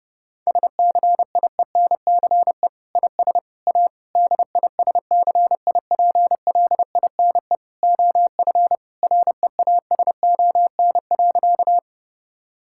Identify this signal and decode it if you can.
Morse code, heard as a keyed tone, about 30 wpm, 700 Hz. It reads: SCIENCE IS A DISCIPLINE OF REASON.
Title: Sentences from Top 1000 Words - Review Part 6 - 30wpm